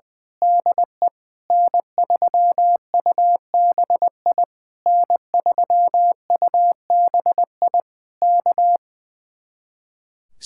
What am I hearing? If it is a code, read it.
Morse code, heard as a keyed tone, about 20 wpm, 700 Hz. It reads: DE N3UBI N3UBI K
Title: Speed-Racing Common QSO Phrases - 20wpm